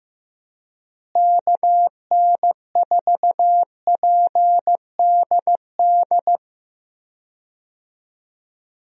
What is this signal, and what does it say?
Morse code, heard as a keyed tone, about 16 wpm, 700 Hz. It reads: KN4PDD
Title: USA Call Signs - 6 Characters - Repeated 2x - 15wpm